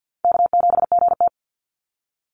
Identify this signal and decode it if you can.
Morse code, heard as a keyed tone, about 50 wpm, 700 Hz. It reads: K7GT
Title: NA SOTA Chaser Callsigns 50wpm